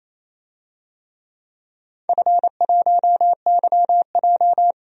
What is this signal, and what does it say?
Morse code, heard as a keyed tone, about 28 wpm, 700 Hz. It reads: F1YJ
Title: DXCC-100 Call Signs - 4 Characters - 28wpm